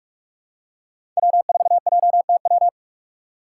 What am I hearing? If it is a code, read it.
Morse code, heard as a keyed tone, about 45 wpm, 700 Hz. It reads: W4JTW
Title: USA Call Signs - 2x Character Spacing - 45wpm